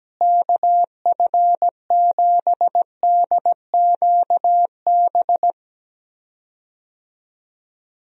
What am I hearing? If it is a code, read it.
Morse code, heard as a keyed tone, about 17 wpm, 700 Hz. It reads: KF7DQB
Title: USA Call Signs - 6 Characters - Repeated 2x - 17wpm